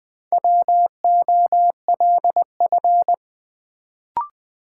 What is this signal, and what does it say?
Morse code, heard as a keyed tone, about 21 wpm, 700 Hz. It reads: WOLF
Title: Build A Word - 2 Letters At A Time - 4-Letter Words - 20wpm